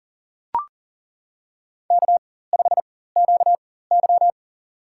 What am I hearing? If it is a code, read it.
Morse code, heard as a keyed tone, about 40 wpm, spaced as for 19 wpm, 700 Hz. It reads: K5QY
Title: USA Call Signs - 4x Character Spacing - 40wpm